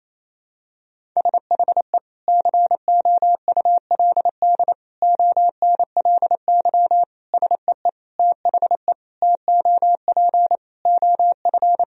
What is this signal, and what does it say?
Morse code, heard as a keyed tone, about 28 wpm, 700 Hz. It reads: SHE COULD ONLY SEE THE TOP OF
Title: Sentences from Top 1000 Words - Review Part 2 - 28wpm